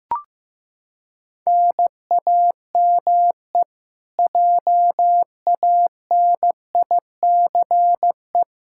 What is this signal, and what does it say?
Morse code, heard as a keyed tone, about 15 wpm, 700 Hz. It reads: NAME JANICE
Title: Practice QSO - Exchange 1 - Beginning - 15wpm